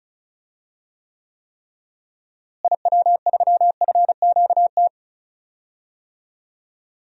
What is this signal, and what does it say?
Morse code, heard as a keyed tone, about 36 wpm, 700 Hz. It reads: IW3FQT
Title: DXCC-100 Call Signs - 6 Characters - Repeated 3x - 35wpm